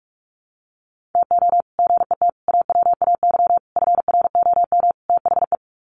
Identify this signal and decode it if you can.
Morse code, heard as a keyed tone, about 44 wpm, 700 Hz. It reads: TO GET AWAY FROM THE
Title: Sets of 5 Words Encore2 45wpm